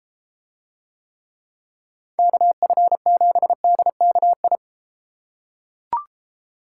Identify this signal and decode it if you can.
Morse code, heard as a keyed tone, about 33 wpm, 700 Hz. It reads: KF7DKI